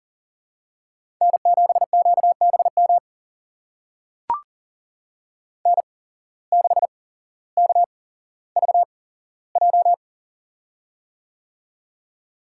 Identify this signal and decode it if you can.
Morse code, heard as a keyed tone, about 40 wpm, spaced as for 17 wpm, 700 Hz. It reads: N7QBM N6KVJ
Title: USA Call Signs - 5 Characters - 8x Character Spacing - 40wpm